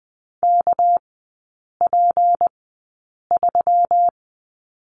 Morse code, heard as a keyed tone, about 20 wpm, spaced as for 8 wpm, 700 Hz. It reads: KP3
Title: 212 Lesson 34e - Number 8 - Three Characters 20wpm